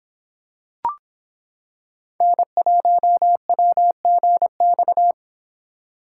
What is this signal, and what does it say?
Morse code, heard as a keyed tone, about 26 wpm, 700 Hz. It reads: N1WGX